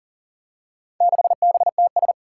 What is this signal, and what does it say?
Morse code, heard as a keyed tone, about 37 wpm, 700 Hz. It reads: 6BTS
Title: Four Challenging Letter-Number Combinations 40wpm